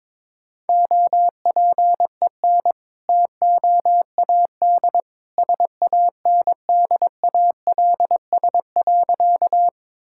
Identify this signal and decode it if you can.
Morse code, heard as a keyed tone, about 22 wpm, 700 Hz. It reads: OPEN TOAD SANDALS.
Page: morse-code-ninja-practice-22wpm